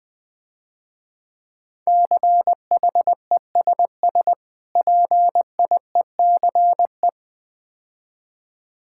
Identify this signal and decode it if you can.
Morse code, heard as a keyed tone, about 20 wpm, 700 Hz. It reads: CHESS PIECE
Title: Difficult Words - Speed-Racing with Hints - 20wpm